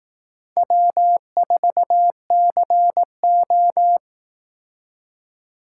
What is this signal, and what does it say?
Morse code, heard as a keyed tone, about 18 wpm, 700 Hz. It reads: W4CO
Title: Speed Racing - USA 1x2 Call Signs 15wpm